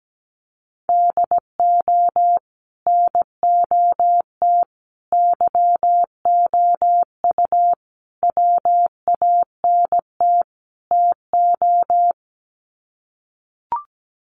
Morse code, heard as a keyed tone, about 17 wpm, 700 Hz. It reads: DO NOT YOU WANT TO
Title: Sets of 5 Words Encore2 17wpm